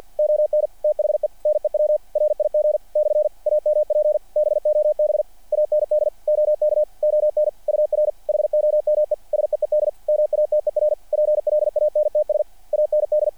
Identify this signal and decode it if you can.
Morse code, heard as a keyed tone, about 48 wpm, 600 Hz. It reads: ON THE NEW RIG / AMP BOB AND OK ON UR HOME SEED MATER PLANTS AND
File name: archived / CRSnet